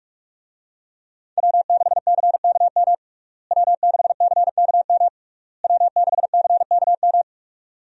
Morse code, heard as a keyed tone, about 43 wpm, 700 Hz. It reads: W6CKM W6CKM W6CKM
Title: USA Call Signs - 5 Characters - Repeated 3x - 45wpm